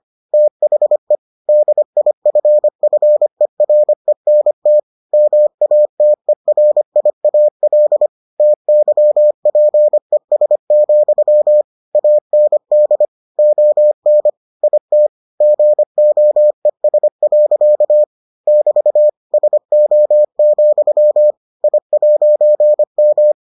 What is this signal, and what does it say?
Morse code, heard as a keyed tone, about 25 wpm, 600 Hz. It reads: THE DIFFERENT MATERIAL TYPES, AND ON IT GOES. <BT> SO, I'M